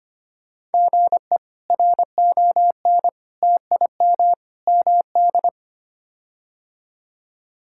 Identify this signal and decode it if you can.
Morse code, heard as a keyed tone, about 25 wpm, 700 Hz. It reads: GE RON TIM MD
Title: K1USN SST QSO Practice - 25wpm